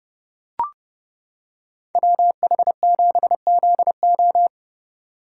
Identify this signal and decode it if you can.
Morse code, heard as a keyed tone, about 29 wpm, 700 Hz. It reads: WH7ZO